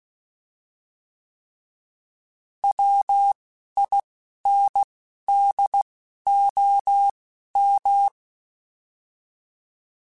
Morse code, heard as a keyed tone, about 16 wpm, spaced as for 11 wpm, 800 Hz. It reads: WINDOM
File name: fromSMs / FromSMs mp3s / Antennas@10